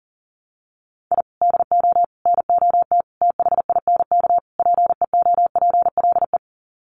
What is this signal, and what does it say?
Morse code, heard as a keyed tone, about 40 wpm, 700 Hz. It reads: I DO NOT THINK PEOPLE
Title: Sets of 5 Words Encore2 40wpm